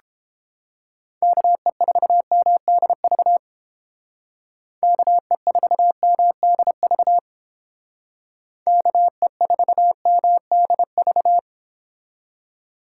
Morse code, heard as a keyed tone, about 30 wpm, 700 Hz. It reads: KE4MDV KE4MDV KE4MDV